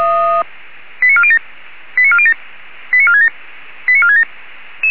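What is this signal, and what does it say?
Five-tone selective call (ZVEI), heard as quick runs of five tones, tones of about 70 ms. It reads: 89407 89407 89467 89467
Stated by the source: ZVEI five-tone call received on an SDR.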